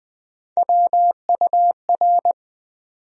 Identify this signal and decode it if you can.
Morse code, heard as a keyed tone, about 21 wpm, 700 Hz. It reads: WUR
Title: USA Call Sign Postfixes - 3 Letters 20wpm